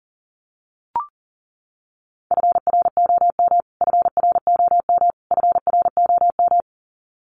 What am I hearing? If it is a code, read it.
Morse code, heard as a keyed tone, about 40 wpm, 700 Hz. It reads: FROM FROM FROM
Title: Top 500 Words Repeated 40wpm